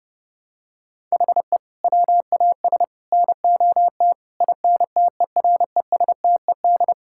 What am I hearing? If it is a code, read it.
Morse code, heard as a keyed tone, about 30 wpm, 700 Hz. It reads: HE WAS NOT INTERESTED
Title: Sentences from Top 1000 Words - Review Part 6 - 30wpm